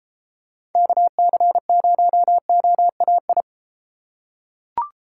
Morse code, heard as a keyed tone, about 32 wpm, 700 Hz. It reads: KC0OAI